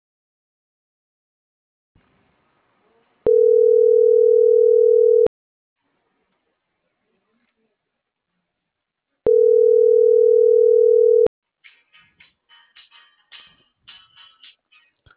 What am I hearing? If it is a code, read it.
Ringback tone (North America), 440 + 480 Hz.